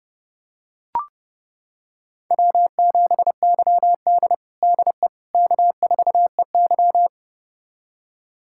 Morse code, heard as a keyed tone, about 30 wpm, 700 Hz. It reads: W7YD DE K4EY
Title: USA Callsign DE Callsign - 4 Characters - 30wpm